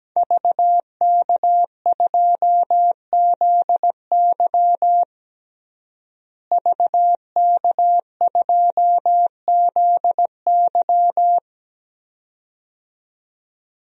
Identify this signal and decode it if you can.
Morse code, heard as a keyed tone, about 17 wpm, 700 Hz. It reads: VK2ZY VK2ZY
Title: DXCC-100 Call Signs - 5 Characters - Repeated 2x - 17wpm